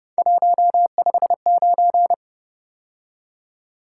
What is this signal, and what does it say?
Morse code, heard as a keyed tone, about 29 wpm, 700 Hz. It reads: 159
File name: Speed-Racing - Three Letter-Number Combinations - 25wpm